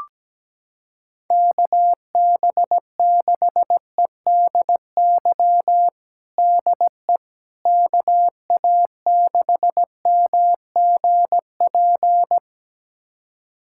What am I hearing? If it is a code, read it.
Morse code, heard as a keyed tone, about 17 wpm, 700 Hz. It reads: KB6EDY DE KA6MGP
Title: USA Callsign DE Callsign - 6 Characters - 17wpm